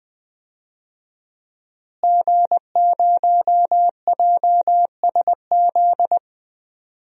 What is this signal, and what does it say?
Morse code, heard as a keyed tone, about 20 wpm, 700 Hz. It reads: G0JSZ
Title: DXCC-100 Call Signs - 5 Characters - Repeated 2x - 20wpm